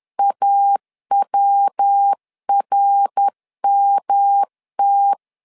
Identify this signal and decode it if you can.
Morse code, heard as a keyed tone, about 10 wpm, 800 Hz. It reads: AWRMT
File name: LEZIONI / 4 Q1 10wpm 2448